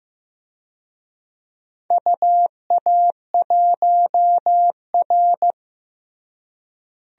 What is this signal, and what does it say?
Morse code, heard as a keyed tone, about 15 wpm, 700 Hz. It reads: UA1R
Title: DXCC-100 Call Signs - 4 Characters - Repeated 2x - 15wpm